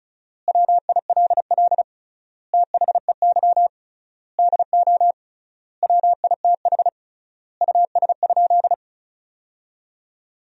Morse code, heard as a keyed tone, about 36 wpm, 700 Hz. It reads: WILL THEY DO WITH US?
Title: 3x Word-Spacing - Sentences from Top 200 Words - 35wpm